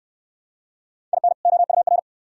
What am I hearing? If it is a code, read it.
Morse code, heard as a keyed tone, about 62 wpm, 690 Hz. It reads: IN OUR